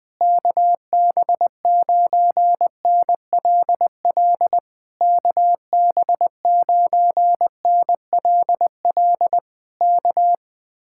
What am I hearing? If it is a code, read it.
Morse code, heard as a keyed tone, about 20 wpm, 700 Hz. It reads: KB9NLL KB9NLL K
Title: Speed-Racing Common QSO Phrases - 20wpm